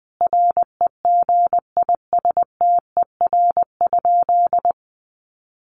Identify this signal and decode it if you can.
Morse code, heard as a keyed tone, about 20 wpm, 700 Hz. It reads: REGISTER?
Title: Sentences from Top 1200 Words 20wpm